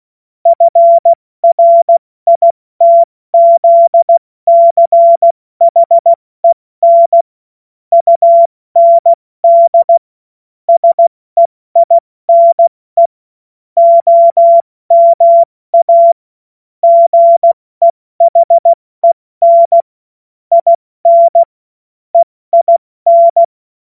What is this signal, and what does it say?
Morse code, heard as a keyed tone, about 16 wpm, 680 Hz. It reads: FRITZCHEN UND SEINE OMA GEHEN IN EIN
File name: Morsekurs / Witze / Witz 28. September 2021, 70 F80